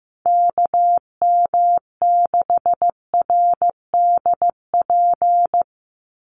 Morse code, heard as a keyed tone, about 15 wpm, 700 Hz. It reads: KM6RDP
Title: USA Call Signs 15wpm